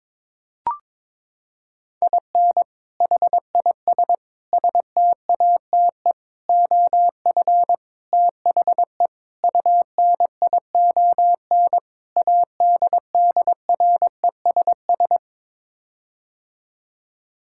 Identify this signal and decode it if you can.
Morse code, heard as a keyed tone, about 22 wpm, 700 Hz. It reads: IN HIS STATE OF THE UNION ADDRESS
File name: practice / 22wpm / Sets of 7 Words - 22wpm